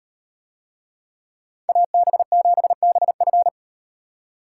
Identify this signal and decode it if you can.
Morse code, heard as a keyed tone, about 38 wpm, 700 Hz. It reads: AB7BF